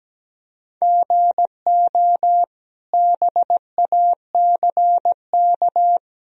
Morse code, heard as a keyed tone, about 17 wpm, 700 Hz. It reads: GO BACK
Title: Sentences from Top 1000 Words - Review Part 1 - 17wpm